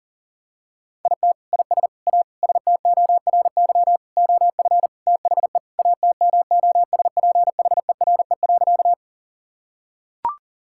Morse code, heard as a keyed tone, about 40 wpm, 700 Hz. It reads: IT IS A STORY OF THE ATMOSPHERE.
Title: Sentences from Top 1600 Words 40wpm